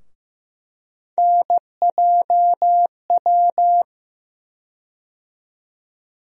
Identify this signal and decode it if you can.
Morse code, heard as a keyed tone, about 15 wpm, 700 Hz. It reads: NJW
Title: USA Call Sign Postfixes - 3 Letters 15wpm